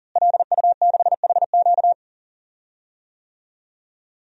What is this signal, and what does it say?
Morse code, heard as a keyed tone, about 40 wpm, 700 Hz. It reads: LU6HQ
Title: DXCC-100 Call Signs - 5 Characters - Repeated 2x - 40wpm